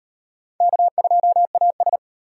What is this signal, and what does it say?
Morse code, heard as a keyed tone, about 36 wpm, 700 Hz. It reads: K2AS